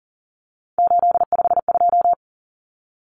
Morse code, heard as a keyed tone, about 40 wpm, 700 Hz. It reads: 852
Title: Common Cut Numbers - 3-Digit 40wpm